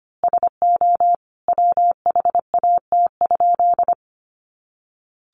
Morse code, heard as a keyed tone, about 25 wpm, 700 Hz. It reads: SO WHAT?
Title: A Sound Of Thunder 25wpm